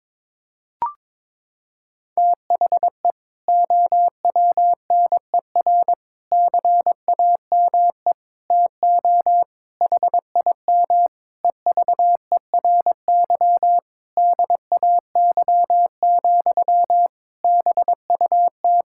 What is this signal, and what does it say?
Morse code, heard as a keyed tone, about 22 wpm, 700 Hz. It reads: THE OWNER CAME TO HIM EVERY DAY, BUT
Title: Sentences from Top 1600 Words 22wpm